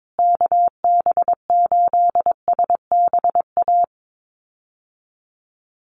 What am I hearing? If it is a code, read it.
Morse code, heard as a keyed tone, about 22 wpm, 700 Hz. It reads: KB8SBA
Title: USA Call Signs Repeated 3x 22wpm